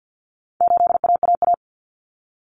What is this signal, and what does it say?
Morse code, heard as a keyed tone, about 50 wpm, 700 Hz. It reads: 8AAA
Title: Common Cut Numbers - 4-Digit 50wpm